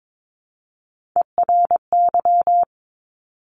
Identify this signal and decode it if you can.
Morse code, heard as a keyed tone, about 22 wpm, 700 Hz. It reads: ERY
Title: Top Three Letter Combinations in Words 22wpm